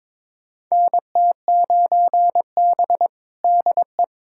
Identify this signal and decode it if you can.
Morse code, heard as a keyed tone, about 22 wpm, 700 Hz. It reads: NT9B DE
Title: USA Callsign DE Callsign - 4 Characters - 22wpm